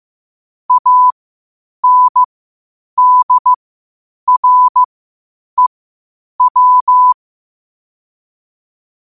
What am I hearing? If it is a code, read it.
Morse code, heard as a keyed tone, about 15 wpm, spaced as for 8 wpm, 1000 Hz. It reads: ANDREW
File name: qotdfiles / 8wpm